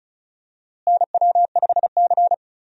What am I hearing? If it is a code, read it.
Morse code, heard as a keyed tone, about 36 wpm, 700 Hz. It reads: NW5C
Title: USA Call Signs - 4 Characters - Repeated 3x - 35wpm